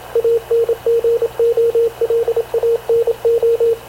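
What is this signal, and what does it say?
Morse code, heard as a keyed tone, about 27 wpm, 460 Hz, with radio noise. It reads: ANGOLANO
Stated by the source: Havana Radio CLA press (PX) broadcast, recorded by Radio Officer Paul Dumesnil aboard mv CNAV QUEST/CZDO.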